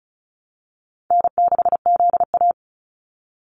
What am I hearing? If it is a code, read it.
Morse code, heard as a keyed tone, about 34 wpm, 700 Hz. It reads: N6ZA
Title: NA SOTA Activator Callsigns 35wpm